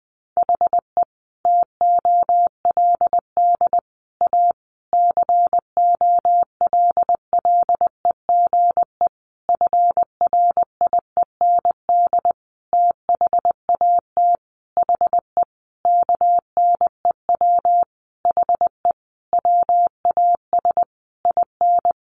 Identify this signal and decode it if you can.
Morse code, heard as a keyed tone, about 19 wpm, 700 Hz. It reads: HE TOLD A COLLEGE FRIEND THAT HE KNEW HE WAS IN
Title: Sentences from Top 1300 Words 20wpm